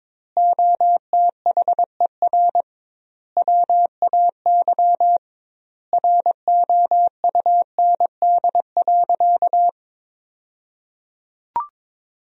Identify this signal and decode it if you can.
Morse code, heard as a keyed tone, about 22 wpm, 700 Hz. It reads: OTHER WAY ROUND.
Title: 2x Word-Spacing - Sentences from Top 200 Words - 22wpm